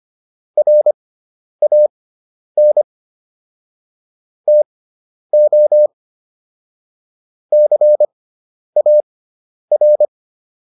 Morse code, heard as a keyed tone, about 25 wpm, spaced as for 10 wpm, 600 Hz. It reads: RAN TO CAR